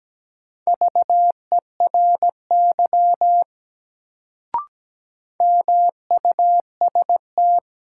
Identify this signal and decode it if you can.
Morse code, heard as a keyed tone, about 17 wpm, 700 Hz. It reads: VERY MUST
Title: Sets of 2 Words - New Word Drill - 17wpm